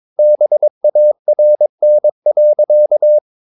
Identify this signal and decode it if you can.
Morse code, heard as a keyed tone, about 22 wpm, 600 Hz. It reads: BARN.
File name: Ferie QRS 21wpm